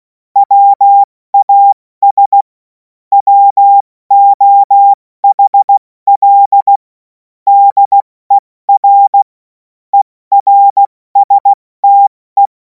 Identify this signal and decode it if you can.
Morse code, heard as a keyed tone, about 16 wpm, spaced as for 14 wpm, 800 Hz. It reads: WAS WOHL DER ERSTE
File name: Morsekurs / Witze / Witz 24. Februar 2026 70 F80